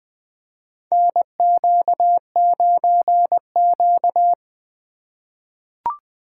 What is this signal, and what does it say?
Morse code, heard as a keyed tone, about 20 wpm, 700 Hz. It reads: NQ9Q
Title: USA Call Signs - 4 Characters - 3x Character Spacing - 20wpm